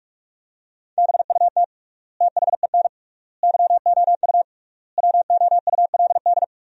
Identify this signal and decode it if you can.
Morse code, heard as a keyed tone, about 46 wpm, 700 Hz. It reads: BUT THEN YOU WOULD
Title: 3x Word-Spacing - Sentences from Top 300 Words - 45wpm